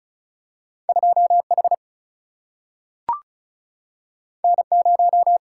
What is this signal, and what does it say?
Morse code, heard as a keyed tone, about 36 wpm, 700 Hz. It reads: 2 H N 0
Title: Single-Letter Single-Number 35wpm